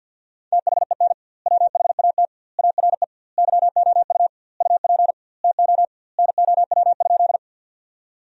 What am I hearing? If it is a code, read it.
Morse code, heard as a keyed tone, about 52 wpm, 700 Hz. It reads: THEN WHAT ARE YOU UP TO NOW?
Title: 2x Word-Spacing - Sentences from Top 100 Words - 50wpm